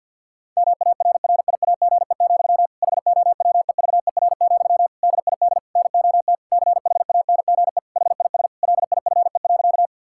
Morse code, heard as a keyed tone, about 50 wpm, 700 Hz. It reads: MARRIAGE, HOWEVER, DID NOT CHANGE HIS LIFE.
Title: Sentences from Top 1000 Words - Review Part 5 - 50wpm